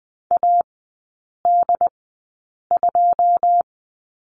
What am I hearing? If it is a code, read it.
Morse code, heard as a keyed tone, about 20 wpm, spaced as for 8 wpm, 690 Hz. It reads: AD2
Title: 081 Lesson 14e - Number 2 - Three Characters 20wpm